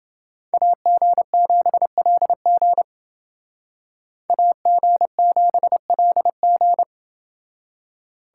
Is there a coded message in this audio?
Yes: Morse code from 0.5 to 6.8 s, heard as a keyed tone, about 28 wpm, 700 Hz. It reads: AG7LG AG7LG